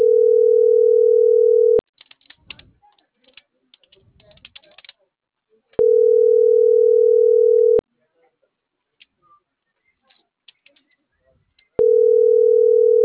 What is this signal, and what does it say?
Ringback tone (North America), 440 + 480 Hz.